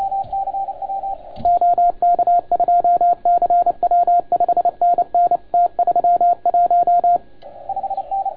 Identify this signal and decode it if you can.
Morse code, heard as a keyed tone, about 28 wpm, 690 Hz, with radio noise. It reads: OK2CW 5NN T31
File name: Archiv / CUC287 / OK1FGD CUC 287